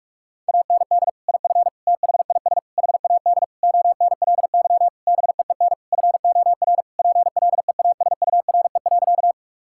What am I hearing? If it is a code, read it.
Morse code, heard as a keyed tone, about 46 wpm, 700 Hz. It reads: AND IF THIS HAD ONLY BEEN FOR PLEASURE.
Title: Sentences from Top 1600 Words 45wpm